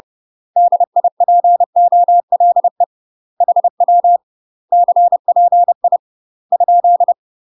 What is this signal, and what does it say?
Morse code, heard as a keyed tone, about 30 wpm, 700 Hz. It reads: DIPOLE HW CPI ?
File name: UK MASTER QSOs TRAINING / 30WPM / UK MASTER QSOs 0014